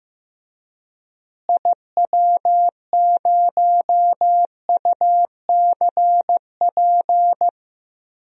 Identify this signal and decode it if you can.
Morse code, heard as a keyed tone, about 15 wpm, 700 Hz. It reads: IW0UCP
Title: DXCC-100 Call Signs - 6 Characters - Repeated 3x - 15wpm